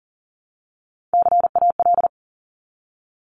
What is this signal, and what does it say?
Morse code, heard as a keyed tone, about 38 wpm, 700 Hz. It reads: CAL
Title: Top Three Letter Combinations in Words 40wpm